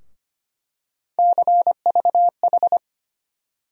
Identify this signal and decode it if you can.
Morse code, heard as a keyed tone, about 25 wpm, 700 Hz. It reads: CVH
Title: USA Call Sign Postfixes - 3 Letters 25wpm